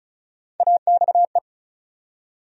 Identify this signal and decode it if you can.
Morse code, heard as a keyed tone, about 34 wpm, 700 Hz. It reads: AXE
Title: USA Call Sign Postfixes - 3 Letters 35wpm